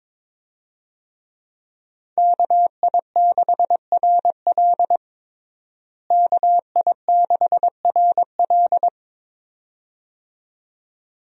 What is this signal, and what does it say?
Morse code, heard as a keyed tone, about 22 wpm, 700 Hz. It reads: KI6RL KI6RL
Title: USA Call Signs - 5 Characters - Repeated 2x - 22wpm